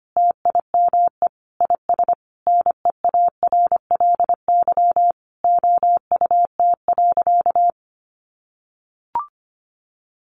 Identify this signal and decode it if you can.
Morse code, heard as a keyed tone, about 25 wpm, 700 Hz. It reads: TIME IS NEARLY OUT.
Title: Sentences from Top 1300 Words 25wpm